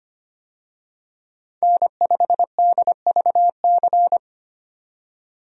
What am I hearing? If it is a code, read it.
Morse code, heard as a keyed tone, about 25 wpm, 700 Hz. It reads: N5DVC
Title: USA Call Signs - 5 Characters - 25wpm